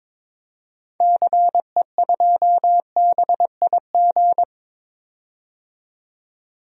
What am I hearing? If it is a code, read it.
Morse code, heard as a keyed tone, about 22 wpm, 700 Hz. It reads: CE2BIG
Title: DXCC-100 Call Signs - 6 Characters - 22wpm